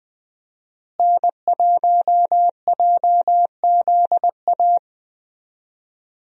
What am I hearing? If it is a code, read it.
Morse code, heard as a keyed tone, about 21 wpm, 700 Hz. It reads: N1JZA
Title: USA Call Signs - 5 Characters - Repeated 2x - 20wpm